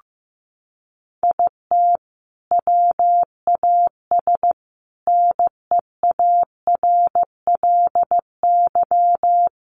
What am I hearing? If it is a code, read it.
Morse code, heard as a keyed tone, about 15 wpm, 700 Hz. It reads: IT WAS NEARLY
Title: Sentences from Top 1300 Words 15wpm